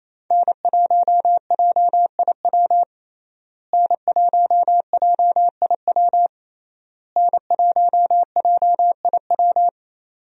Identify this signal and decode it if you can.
Morse code, heard as a keyed tone, about 28 wpm, 700 Hz. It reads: N1JIW N1JIW N1JIW
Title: USA Call Signs - 5 Characters - Repeated 3x - 28wpm